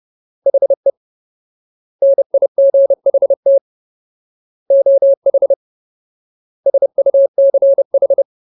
Morse code, heard as a keyed tone, about 30 wpm, 550 Hz. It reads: HE NIGHT OH SUCH
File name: Ultimate Word List / 30WPM Ultimate